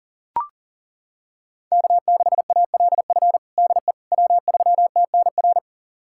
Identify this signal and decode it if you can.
Morse code, heard as a keyed tone, about 40 wpm, 700 Hz. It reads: K6ALF DE W3TNR
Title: USA Callsign DE Callsign - 5 Characters - 40wpm